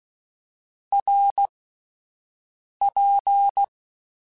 Morse code, heard as a keyed tone, about 16 wpm, spaced as for 5 wpm, 800 Hz. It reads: RP
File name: Short Runs / Short Runs-mp3s / Balanced @10